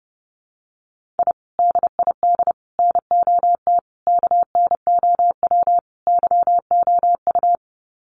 Morse code, heard as a keyed tone, about 30 wpm, 700 Hz. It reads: I DID NOT KNOW YOU
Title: Sets of 5 Words Encore2 30wpm